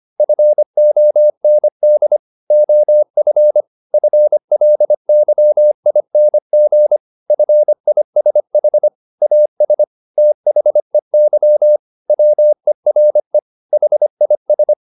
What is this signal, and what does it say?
Morse code, heard as a keyed tone, about 25 wpm, 600 Hz. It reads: FOND OF FLYING FISH AS THEY WERE HIS
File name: the old man and the sea part 1 25wpm